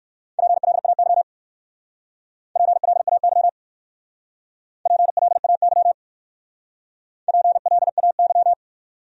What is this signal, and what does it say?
Morse code, heard as a keyed tone, about 55 wpm, 700 Hz. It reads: PLAY PLAY PLAY PLAY
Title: Speed-Racing Top 200 Words 45wpm